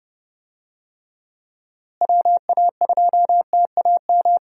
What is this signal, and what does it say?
Morse code, heard as a keyed tone, about 30 wpm, 700 Hz. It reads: WA2TAM
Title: USA Call Signs - 6 Characters - Repeated 3x - 30wpm